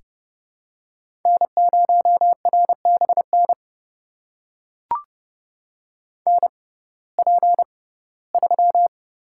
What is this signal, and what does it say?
Morse code, heard as a keyed tone, about 29 wpm, spaced as for 18 wpm, 700 Hz. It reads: N0RBN NP3
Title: USA Call Signs - 5 Characters - 6x Character Spacing - 30wpm